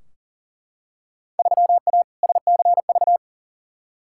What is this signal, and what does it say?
Morse code, heard as a keyed tone, about 40 wpm, 700 Hz. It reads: 3A SCV
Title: Field Day Exchanges - 40wpm